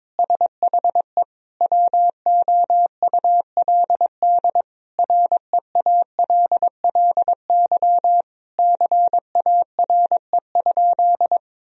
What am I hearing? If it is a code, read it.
Morse code, heard as a keyed tone, about 22 wpm, 700 Hz. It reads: SHE WOULD REALLY CARE?
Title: Sentences from Top 1000 Words - Review Part 3 - 22wpm